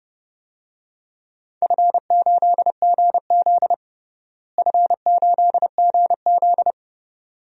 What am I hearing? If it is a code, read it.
Morse code, heard as a keyed tone, about 30 wpm, 700 Hz. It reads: F8GZ F8GZ
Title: DXCC-100 Call Signs - 4 Characters - Repeated 2x - 30wpm